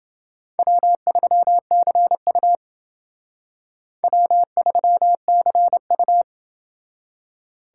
Morse code, heard as a keyed tone, about 29 wpm, 700 Hz. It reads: W3CU W3CU